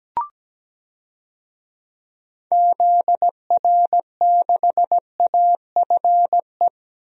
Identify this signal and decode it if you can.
Morse code, heard as a keyed tone, about 17 wpm, 700 Hz. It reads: ZR6AFE
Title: DXCC-100 Call Signs - 6 Characters - Repeated 2x - 17wpm